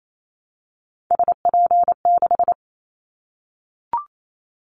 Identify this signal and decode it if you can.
Morse code, heard as a keyed tone, about 28 wpm, 700 Hz. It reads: SP6
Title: Callsign Prefixes for Top 25 DXCC - 28wpm